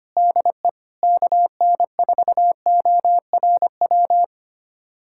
Morse code, heard as a keyed tone, about 25 wpm, 700 Hz. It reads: DE KN4ORW
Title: USA Callsign DE Callsign - 6 Characters - 25wpm